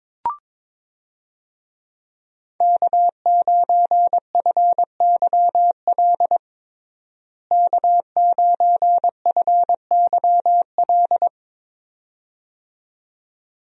Morse code, heard as a keyed tone, about 22 wpm, 700 Hz. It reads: K9FYL K9FYL
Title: USA Call Signs - 5 Characters - Repeated 2x - 22wpm